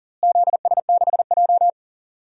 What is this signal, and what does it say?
Morse code, heard as a keyed tone, about 40 wpm, 700 Hz. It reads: ZS6J
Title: DXCC-100 Call Signs - 4 Characters - Repeated 3x - 40wpm